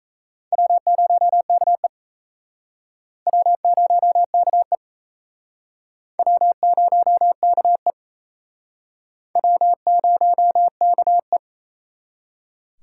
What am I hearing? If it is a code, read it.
Morse code, heard as a keyed tone, about 34 wpm, 700 Hz. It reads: W0KE W0KE W0KE W0KE
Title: Speed Racing - USA 1x2 Call Signs 28wpm